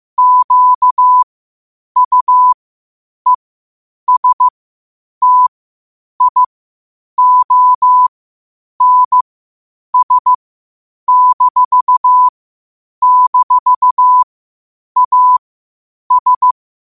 Morse code, heard as a keyed tone, about 15 wpm, spaced as for 8 wpm, 1000 Hz. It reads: QUESTIONS--AS
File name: 8wpm